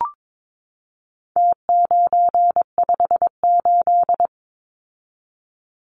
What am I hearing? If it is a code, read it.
Morse code, heard as a keyed tone, about 22 wpm, 700 Hz. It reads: T958
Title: Numbers - Contest Style - 4-Digit 22wpm